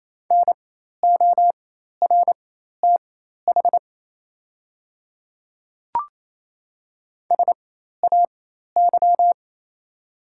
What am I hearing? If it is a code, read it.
Morse code, heard as a keyed tone, about 27 wpm, spaced as for 13 wpm, 690 Hz. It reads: NORTH SAY
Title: 4x Character Spacing - Top 300 Words - 28wpm